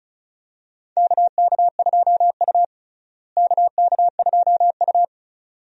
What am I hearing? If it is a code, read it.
Morse code, heard as a keyed tone, about 36 wpm, 700 Hz. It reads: KK2U KK2U
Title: USA Call Signs - 4 Characters - Repeated 2x - 35wpm